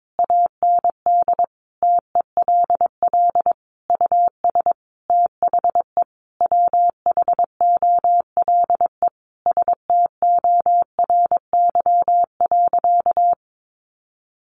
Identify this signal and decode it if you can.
Morse code, heard as a keyed tone, about 22 wpm, 700 Hz. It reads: AND TELL US THE WHOLE STORY.
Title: Sentences from Top 1100 Words 22wpm